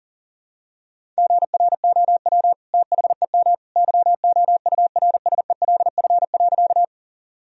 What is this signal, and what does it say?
Morse code, heard as a keyed tone, about 40 wpm, 700 Hz. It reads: GROW THEM YOURSELF.
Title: Sentences from Top 1000 Words - Review Part 1 - 40wpm